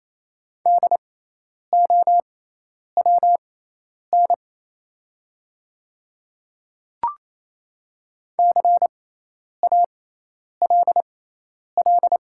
Morse code, heard as a keyed tone, about 28 wpm, spaced as for 10 wpm, 700 Hz. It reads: DOWN CALL
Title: 6x Character Spacing - Top 300 Words - 28wpm